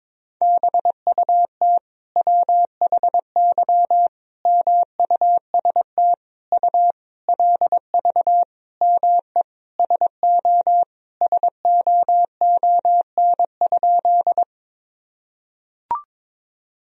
Morse code, heard as a keyed tone, about 22 wpm, 700 Hz. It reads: BUT WHY MUST U LV ME SO SOON?
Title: Sentences from Top 400 Words and Abbreviations 22wpm